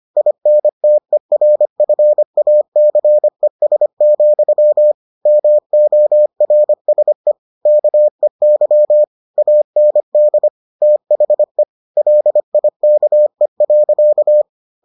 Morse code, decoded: INTERFACES, MORSE KEY AND THE LIKE.

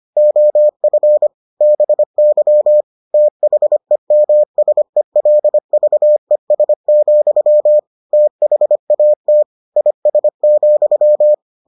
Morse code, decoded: OF BY THEMSELVES, THAT IS,

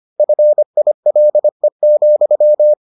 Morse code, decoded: FILE,

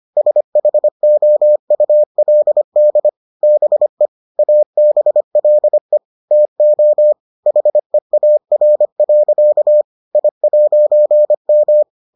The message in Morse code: SHOULD BE ABLE TO HEAR. I'M

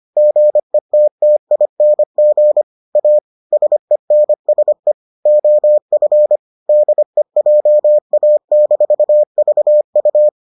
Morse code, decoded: GETTING A SENSE OF DEJA-VU

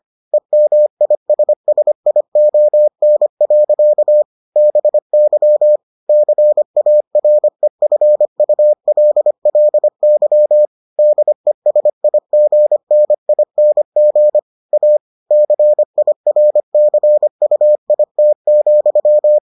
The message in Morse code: EMISSION. BY CAREFULLY DESIGNING A CIRCUIT,

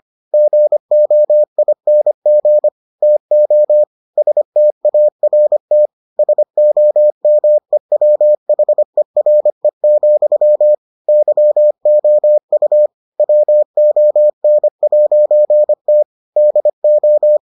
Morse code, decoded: GOING TO START SOMEWHERE, YOU WON'T DO

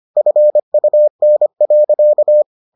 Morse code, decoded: FUN.

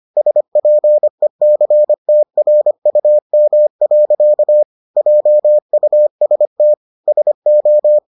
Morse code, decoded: SPECTRUM. JUST SO

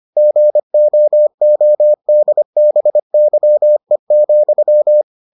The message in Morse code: GOODBYE,

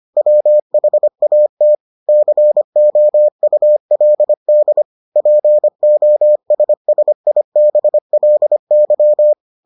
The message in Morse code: WHAT COULD POSSIBLY

